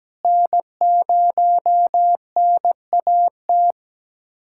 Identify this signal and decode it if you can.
Morse code, heard as a keyed tone, about 17 wpm, 700 Hz. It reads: N0NAT
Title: USA Callsign DE Callsign - 5 Characters - 17wpm